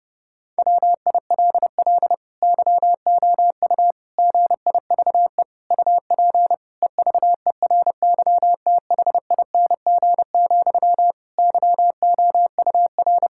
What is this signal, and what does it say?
Morse code, heard as a keyed tone, about 30 wpm, 700 Hz. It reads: WILL YOU GIVE UP EVERYTHING, YOUR
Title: Sentences from Top 1000 Words - Review Part 5 - 30wpm